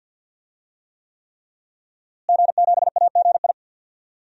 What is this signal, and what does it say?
Morse code, heard as a keyed tone, about 52 wpm, 700 Hz. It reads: G7AGI
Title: DXCC-100 Call Signs - 5 Characters - Repeated 3x - 50wpm